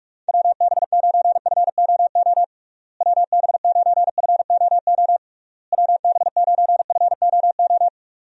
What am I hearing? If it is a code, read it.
Morse code, heard as a keyed tone, about 46 wpm, 700 Hz. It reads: WB9FOO WB9FOO WB9FOO
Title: USA Call Signs - 6 Characters - Repeated 3x - 45wpm